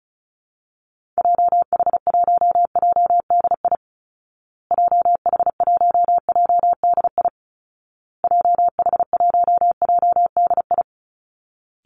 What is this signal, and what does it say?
Morse code, heard as a keyed tone, about 34 wpm, 700 Hz. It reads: JH1JDI JH1JDI JH1JDI
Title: Call Signs Repeated 2x - 35wpm